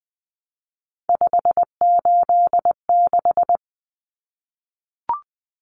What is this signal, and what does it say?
Morse code, heard as a keyed tone, about 20 wpm, 700 Hz. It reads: 586
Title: Common Cut Numbers - 3-Digit 20wpm